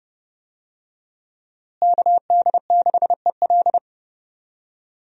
Morse code, heard as a keyed tone, about 30 wpm, 700 Hz. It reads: KD6EL
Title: USA Call Signs - 5 Characters - Repeated 2x - 30wpm